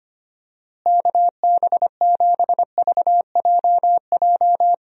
Morse code, heard as a keyed tone, about 25 wpm, 700 Hz. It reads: KB7VJJ
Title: USA Call Signs - 6 Characters - Repeated 3x - 25wpm